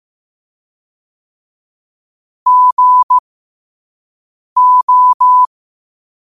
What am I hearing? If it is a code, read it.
Morse code, heard as a keyed tone, about 15 wpm, spaced as for 5 wpm, 1000 Hz. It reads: GO
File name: qotdfiles / 5wpm